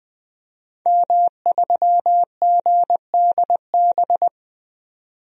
Morse code, heard as a keyed tone, about 20 wpm, 700 Hz. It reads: M3GDB
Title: DXCC-100 Call Signs - 5 Characters - 20wpm